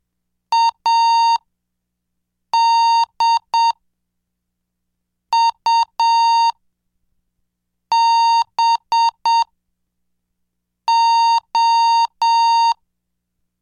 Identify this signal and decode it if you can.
Morse code, heard as a keyed tone, about 7 wpm, spaced as for 4 wpm, 920 Hz. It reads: ADUBO